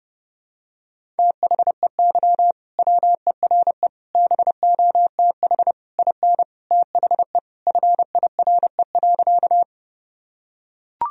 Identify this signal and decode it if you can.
Morse code, heard as a keyed tone, about 30 wpm, 700 Hz. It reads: THEY WERE BOTH IN THE FIRE.
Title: Sentences from Top 1000 Words - Review Part 1 - 30wpm